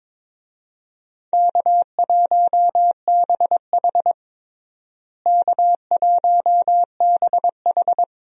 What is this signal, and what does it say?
Morse code, heard as a keyed tone, about 22 wpm, 700 Hz. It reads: K1BH K1BH
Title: USA Call Signs - 4 Characters - Repeated 2x - 22wpm